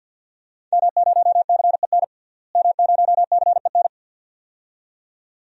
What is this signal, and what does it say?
Morse code, heard as a keyed tone, about 52 wpm, 700 Hz. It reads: M0CEN M0CEN
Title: DXCC-100 Call Signs - 5 Characters - Repeated 2x - 50wpm